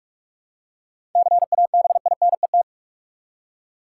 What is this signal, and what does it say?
Morse code, heard as a keyed tone, about 47 wpm, 700 Hz. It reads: CABINET
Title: Seven Letter Words - 45wpm